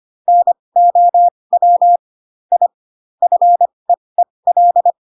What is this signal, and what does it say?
Morse code, heard as a keyed tone, about 25 wpm, spaced as for 20 wpm, 700 Hz. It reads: NOW I FEEL
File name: Pinocchio / Pinocchio ch7 15 25 ik1hge